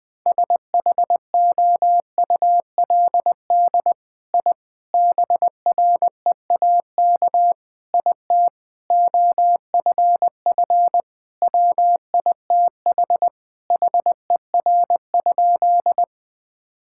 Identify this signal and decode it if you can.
Morse code, heard as a keyed tone, about 21 wpm, 700 Hz. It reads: SHOULD I BREAK IT OFF WITH HER?
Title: Sentences from Top 1000 Words - Review Part 6 - 20wpm